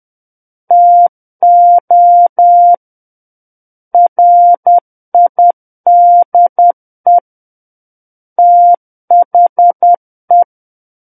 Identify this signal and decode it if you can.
Morse code, heard as a keyed tone, about 10 wpm, 700 Hz. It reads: TO RIDE THE
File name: Book One / 01-Chapter 1 - 10 WPM